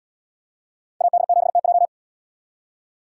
Morse code, heard as a keyed tone, about 71 wpm, 700 Hz. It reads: AD7TJ